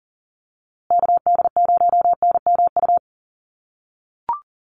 Morse code, heard as a keyed tone, about 40 wpm, 700 Hz. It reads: KD0NMU
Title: USA Call Signs Repeated 2x 40wpm